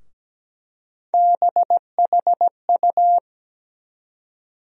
Morse code, heard as a keyed tone, about 17 wpm, 700 Hz. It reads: BHU
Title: USA Call Sign Postfixes - 3 Letters 17wpm